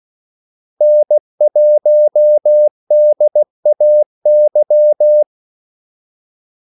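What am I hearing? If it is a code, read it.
Morse code, heard as a keyed tone, about 16 wpm, 600 Hz. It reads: N1DAY
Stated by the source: Morse call sign practice at 16 wpm.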